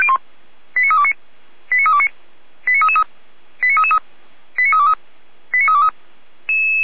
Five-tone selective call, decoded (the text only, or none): 89329 89329 89333 89323 89323